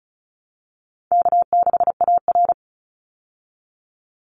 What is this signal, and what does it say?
Morse code, heard as a keyed tone, about 34 wpm, 700 Hz. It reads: K6AR
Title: Easy Call Signs Repeated 3x - 35wpm